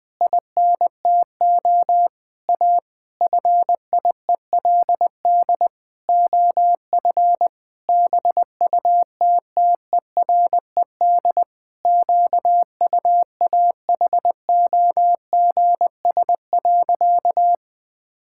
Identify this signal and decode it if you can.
Morse code, heard as a keyed tone, about 20 wpm, 700 Hz. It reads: INTO A FIELD OF BUTTERED QUAHOGS.
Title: Pangram Challenge v2 - 20wpm